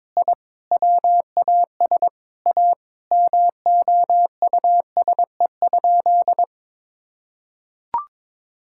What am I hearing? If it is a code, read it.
Morse code, heard as a keyed tone, about 22 wpm, 700 Hz. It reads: I WAS A MOUSE?
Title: Sentences from Top 1000 Words - Review Part 5 - 22wpm